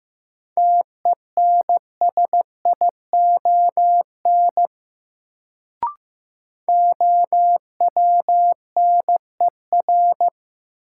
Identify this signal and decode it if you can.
Morse code, heard as a keyed tone, about 15 wpm, 700 Hz. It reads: TENSION OWNER
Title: Sentences from Top 1600 Words - New Word Drill Alt - 15wpm